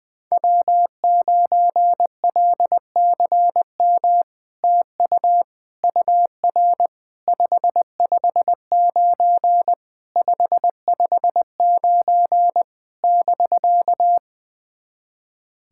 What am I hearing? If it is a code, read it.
Morse code, heard as a keyed tone, about 20 wpm, 690 Hz. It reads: W9LCM TU UR 559 559 <BK>
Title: SOTA QSO Practice 20wpm